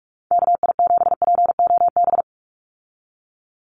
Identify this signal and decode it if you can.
Morse code, heard as a keyed tone, about 46 wpm, 700 Hz. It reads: KI7POB
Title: USA Call Signs 45wpm